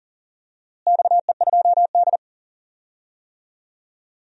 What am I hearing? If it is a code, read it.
Morse code, heard as a keyed tone, about 40 wpm, 700 Hz. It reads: XE2D
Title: DXCC-100 Call Signs - 4 Characters - Repeated 3x - 40wpm